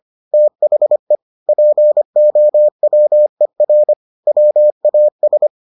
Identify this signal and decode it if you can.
Morse code, heard as a keyed tone, about 25 wpm, 600 Hz. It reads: THE POWER WAS